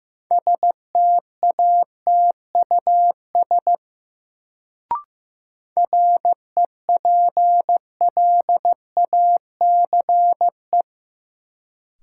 Morse code, heard as a keyed tone, about 15 wpm, 700 Hz. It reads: STATUS REPLACE
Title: Sentences from Top 1400 Words - New Word Drill Alt - 15wpm